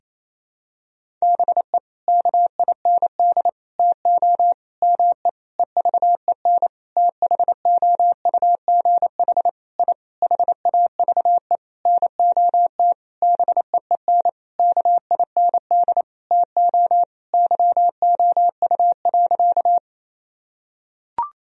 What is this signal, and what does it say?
Morse code, heard as a keyed tone, about 28 wpm, 700 Hz. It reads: BE KIND TO ME EVEN THOUGH I HAVE NOT BEEN KIND TO YOU.
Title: Sentences from Top 1000 Words - Review Part 2 - 28wpm